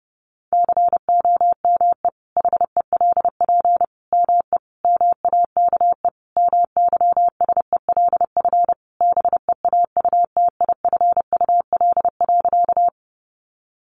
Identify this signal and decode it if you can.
Morse code, heard as a keyed tone, about 30 wpm, 700 Hz. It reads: COME HELP ME MAKE MYSELF BEAUTIFUL.
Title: Sentences from Top 1300 Words 30wpm